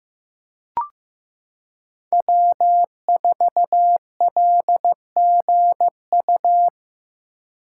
Morse code, heard as a keyed tone, about 15 wpm, 700 Hz. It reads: W4LGU
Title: Code Groups - 5 Letter:Numbers - 15wpm